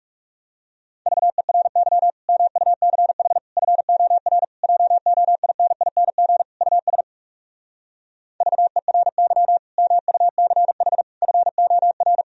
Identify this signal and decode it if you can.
Morse code, heard as a keyed tone, about 40 wpm, 700 Hz. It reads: VERY MUCH FOR JOINING US VERY MUCH FOR